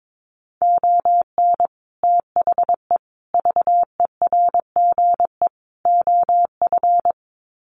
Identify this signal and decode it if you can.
Morse code, heard as a keyed tone, about 22 wpm, 700 Hz. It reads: ON THE VERGE OF
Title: Sets of 4 Words 22wpm